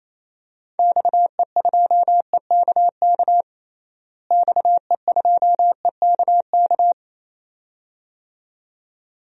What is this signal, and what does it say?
Morse code, heard as a keyed tone, about 27 wpm, 700 Hz. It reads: XE2EKK XE2EKK
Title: DXCC-100 Call Signs - 6 Characters - Repeated 2x - 28wpm